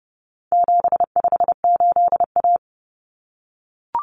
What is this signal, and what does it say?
Morse code, heard as a keyed tone, about 30 wpm, 700 Hz. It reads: 758A
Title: Common Cut Numbers - 4-Digit 30wpm